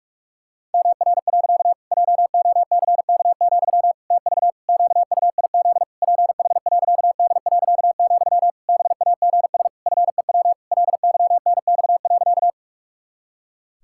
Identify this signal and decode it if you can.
Morse code, heard as a keyed tone, about 41 wpm, 700 Hz. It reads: MR. JOCK, TV QUIZ PH.D., BAGS FEW LYNX.
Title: Pangram Challenge v2 - 45wpm